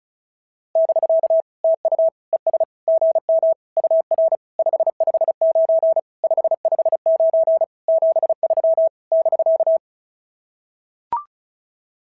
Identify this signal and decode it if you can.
Morse code, heard as a keyed tone, about 36 wpm, 690 Hz. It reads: <BK> TU ES GM UR 559 559 73 <BK>
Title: SOTA QSO Practice 35wpm